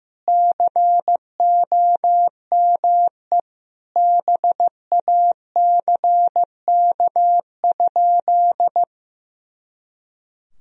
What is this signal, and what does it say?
Morse code, heard as a keyed tone, about 16 wpm, 700 Hz. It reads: COME BACK?
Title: Sentences from Top 300 Words and Abbreviations 15wpm